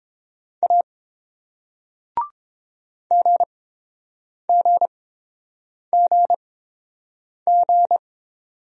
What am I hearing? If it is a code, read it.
Morse code, heard as a keyed tone, about 29 wpm, spaced as for 7 wpm, 700 Hz. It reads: A GGGG